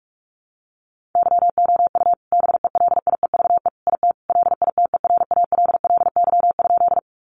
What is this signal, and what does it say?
Morse code, heard as a keyed tone, about 44 wpm, 700 Hz. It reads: YOU BELIEVE IT LITERALLY?
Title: Sentences from Top 1400 Words 45wpm